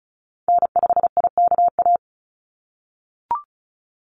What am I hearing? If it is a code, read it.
Morse code, heard as a keyed tone, about 34 wpm, 700 Hz. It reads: N5IKA
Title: USA Call Signs 35wpm